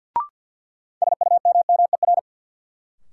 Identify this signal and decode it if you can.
Morse code, heard as a keyed tone, about 52 wpm, 700 Hz. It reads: SUMMER
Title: Sentences from Top 1000 Words - Review Part 6 - New Word Drill - 50wpm